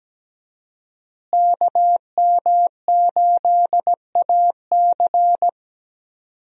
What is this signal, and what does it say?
Morse code, heard as a keyed tone, about 17 wpm, 700 Hz. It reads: KM8AC
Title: USA Call Signs - 5 Characters - Repeated 3x - 17wpm